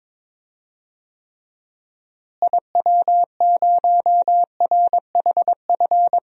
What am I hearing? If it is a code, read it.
Morse code, heard as a keyed tone, about 21 wpm, 700 Hz. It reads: IW0RHF
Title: DXCC-100 Call Signs - 6 Characters - Repeated 3x - 22wpm